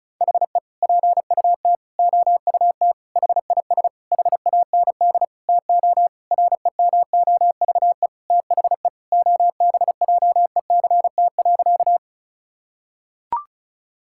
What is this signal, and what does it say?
Morse code, heard as a keyed tone, about 36 wpm, 700 Hz. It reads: HE PUT OUT HIS HAND TO REMOVE THE OBJECT.
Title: Sentences from Top 1000 Words - Review Part 5 - 35wpm